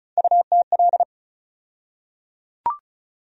Morse code, heard as a keyed tone, about 36 wpm, 700 Hz. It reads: UTL
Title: USA Call Sign Postfixes - 3 Letters 35wpm